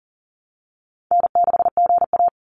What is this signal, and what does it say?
Morse code, heard as a keyed tone, about 40 wpm, 700 Hz. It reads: N6GA
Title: Easy Call Signs Repeated 2x - 40wpm